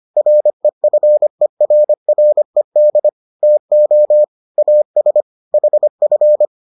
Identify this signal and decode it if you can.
Morse code, heard as a keyed tone, about 25 wpm, 600 Hz. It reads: REFERRED TO AS HF